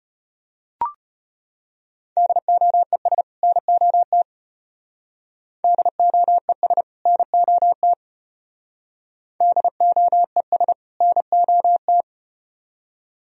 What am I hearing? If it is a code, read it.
Morse code, heard as a keyed tone, about 34 wpm, 700 Hz. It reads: DOES NOT DOES NOT DOES NOT